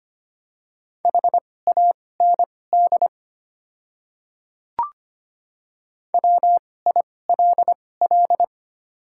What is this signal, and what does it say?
Morse code, heard as a keyed tone, about 25 wpm, spaced as for 18 wpm, 700 Hz. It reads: HAND WILL
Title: 2x Character Spacing - Top 200 Words - 25wpm